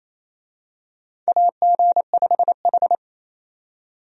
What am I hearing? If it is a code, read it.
Morse code, heard as a keyed tone, about 27 wpm, 700 Hz. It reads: AG5H
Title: USA Call Signs - 4 Characters - 28wpm